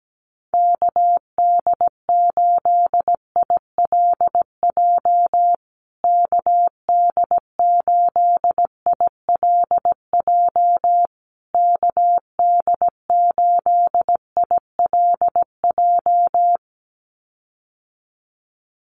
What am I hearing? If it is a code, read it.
Morse code, heard as a keyed tone, about 17 wpm, 700 Hz. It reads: KD8ILJ KD8ILJ KD8ILJ
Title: USA Call Signs Repeated 3x 17wpm